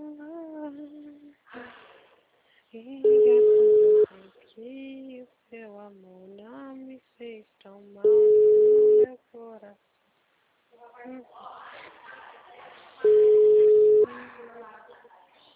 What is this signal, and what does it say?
Ringback tone (Europe, 425 Hz), with speech.